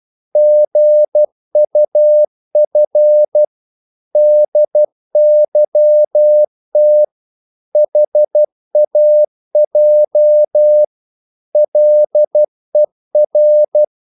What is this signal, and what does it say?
Morse code, heard as a keyed tone, about 12 wpm, 600 Hz. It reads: GUF DYT HAJ LER
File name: Korte ord DK 12wpm